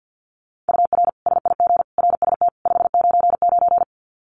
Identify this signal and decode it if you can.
Morse code, heard as a keyed tone, about 50 wpm, 700 Hz. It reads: UR SIG RST 599
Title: Common QSO Phrases 50wpm